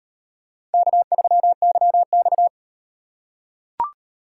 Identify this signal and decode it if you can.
Morse code, heard as a keyed tone, about 34 wpm, 700 Hz. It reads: K3YX